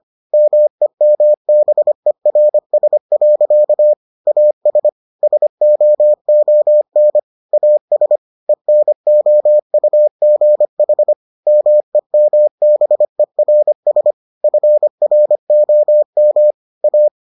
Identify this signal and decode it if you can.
Morse code, heard as a keyed tone, about 25 wpm, 600 Hz. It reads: MEMBERS. AS SOON AS ENOUGH MEMBERS FROM A